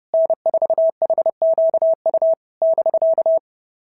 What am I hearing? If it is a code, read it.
Morse code, heard as a keyed tone, about 29 wpm, 690 Hz. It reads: N4HQU <BK>
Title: SOTA QSO Practice 30wpm